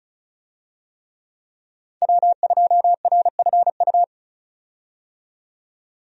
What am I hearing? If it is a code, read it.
Morse code, heard as a keyed tone, about 36 wpm, 700 Hz. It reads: W2RFU
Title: USA Call Signs - 5 Characters - 35wpm